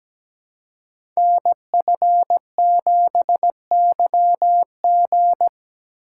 Morse code, heard as a keyed tone, about 17 wpm, 700 Hz. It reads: NF7YG
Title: Code Groups - 5 Letter:Numbers - 17wpm